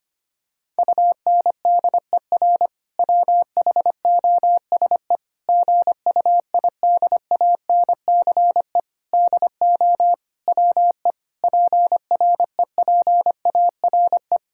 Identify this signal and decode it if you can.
Morse code, heard as a keyed tone, about 25 wpm, 700 Hz. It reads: UNDER WHOSE GUIDANCE DO WE PREPARE
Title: Sentences from Top 1600 Words 25wpm